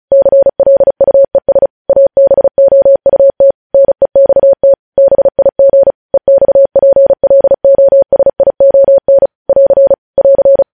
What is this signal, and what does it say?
Morse code, heard as a keyed tone, about 36 wpm, 550 Hz. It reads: CLUES ABOUT NEXT BIG EXPLOSION <AR> <AR>
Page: news headlines in morse code at 35 wpm